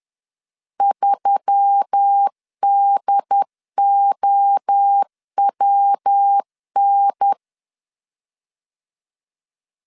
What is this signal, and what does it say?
Morse code, heard as a keyed tone, about 10 wpm, 800 Hz. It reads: 3DOWN
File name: LEZIONI / 6 PKX0 / 6 P 10wpm 2432